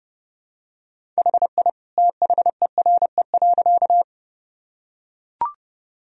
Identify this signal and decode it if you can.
Morse code, heard as a keyed tone, about 29 wpm, 700 Hz. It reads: HI THERE.
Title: Sentences from Top 1000 Words - Review Part 3 - 30wpm